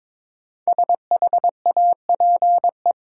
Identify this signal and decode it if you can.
Morse code, heard as a keyed tone, about 22 wpm, 700 Hz. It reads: SHAPE
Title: Sets of 5 Words Encore - New Word Drill - 22wpm